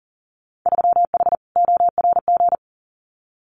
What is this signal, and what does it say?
Morse code, heard as a keyed tone, about 40 wpm, 700 Hz. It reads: 3H ORG
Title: Winter Field Day Exchanges Encore 40wpm